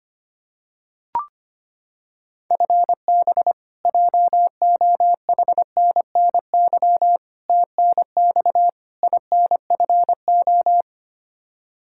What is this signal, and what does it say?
Morse code, heard as a keyed tone, about 25 wpm, 700 Hz. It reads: FB JOHNNY TNX INFO
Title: Practice QSO - Exchange 3 - More Info - 25wpm